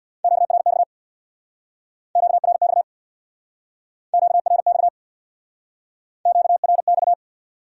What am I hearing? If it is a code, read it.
Morse code, heard as a keyed tone, about 65 wpm, 700 Hz. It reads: QRX QRX QRX QRX